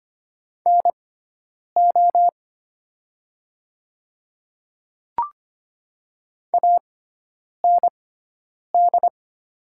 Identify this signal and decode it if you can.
Morse code, heard as a keyed tone, about 25 wpm, spaced as for 9 wpm, 700 Hz. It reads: NO AND
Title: 6x Character Spacing - Top 300 Words - 25wpm